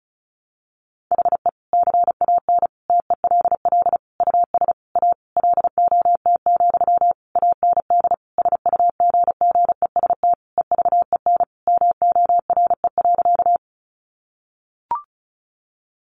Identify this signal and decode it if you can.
Morse code, heard as a keyed tone, about 34 wpm, 700 Hz. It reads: HE CAN TELL US A LOT, AND SUGGEST EVEN MORE.
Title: Sentences from Top 1300 Words 35wpm